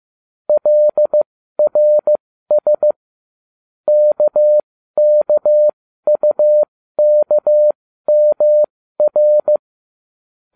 Morse code, heard as a keyed tone, about 15 wpm, spaced as for 12 wpm, 600 Hz. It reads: LRS KKUKMR